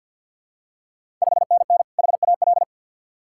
Morse code, heard as a keyed tone, about 53 wpm, 700 Hz. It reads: 5NN HAL
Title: Ontario QSO Party Exchange 50wpm